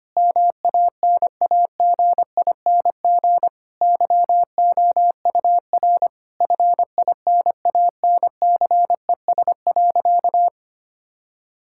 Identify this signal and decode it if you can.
Morse code, heard as a keyed tone, about 25 wpm, 700 Hz. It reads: MANAGING YOUR FINANCES.
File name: practice / 25wpm / Difficult Sentences with Hints - 25wpm